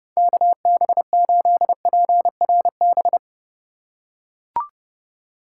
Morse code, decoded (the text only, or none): KB8PRB